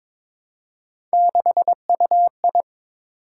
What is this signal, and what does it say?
Morse code, heard as a keyed tone, about 22 wpm, 700 Hz. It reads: 6UI
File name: practice / 22wpm / Three Letter-Number Combinations - 22wpm